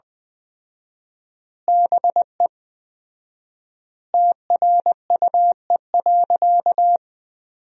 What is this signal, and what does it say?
Morse code, heard as a keyed tone, about 21 wpm, 700 Hz. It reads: BE TRUE.
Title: 4x Word-Spacing - Sentences from Top 400 Words - 20wpm